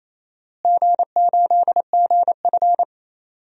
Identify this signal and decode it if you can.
Morse code, heard as a keyed tone, about 28 wpm, 700 Hz. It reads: G8GF
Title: DXCC-100 Call Signs - 4 Characters - Repeated 2x - 28wpm